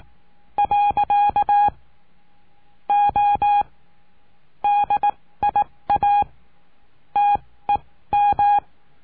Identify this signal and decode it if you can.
Morse code, heard as a keyed tone, about 18 wpm, spaced as for 15 wpm, 820 Hz. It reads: . O DIA TEM
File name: cursos / CURSO DE TELEGRAFIA / Telegrafia - Trilha 20